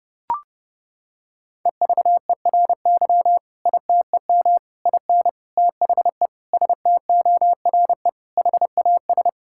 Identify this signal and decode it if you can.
Morse code, heard as a keyed tone, about 30 wpm, 700 Hz. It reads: EVERY ITEM IN THE STORE HAS
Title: Sentences from Top 1000 Words - Review Part 6 - 30wpm